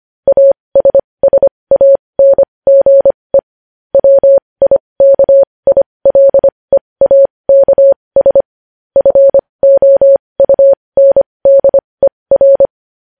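Morse code, heard as a keyed tone, about 25 wpm, spaced as for 20 wpm, 550 Hz. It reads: ASSANGE WIKILEAKS FOUNDER
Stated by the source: Morse news headlines practice at 20 wpm.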